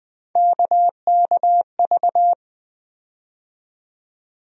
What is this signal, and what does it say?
Morse code, heard as a keyed tone, about 20 wpm, 700 Hz. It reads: KKV
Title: USA Call Sign Postfixes - 3 Letters 20wpm